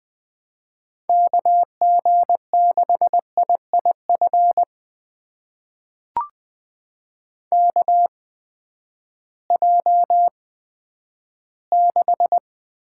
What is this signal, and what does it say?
Morse code, heard as a keyed tone, about 20 wpm, spaced as for 11 wpm, 700 Hz. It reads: KG6IIF KJ6
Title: USA Call Signs - 6 Characters - 8x Character Spacing - 20wpm